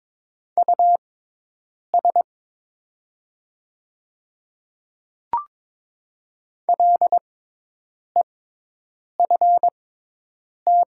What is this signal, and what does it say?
Morse code, heard as a keyed tone, about 22 wpm, spaced as for 8 wpm, 700 Hz. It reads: US LEFT
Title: 6x Character Spacing - Top 300 Words - 22wpm